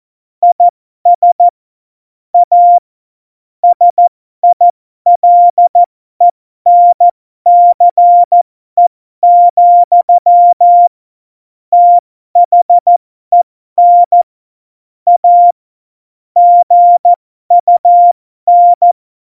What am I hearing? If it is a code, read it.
Morse code, heard as a keyed tone, about 14 wpm, spaced as for 12 wpm, 700 Hz. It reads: IS A SILENCE, THEN A GUN